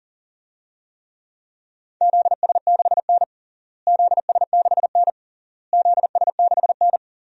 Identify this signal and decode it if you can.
Morse code, heard as a keyed tone, about 40 wpm, 700 Hz. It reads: ZS6N ZS6N ZS6N
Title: DXCC-100 Call Signs - 4 Characters - Repeated 3x - 40wpm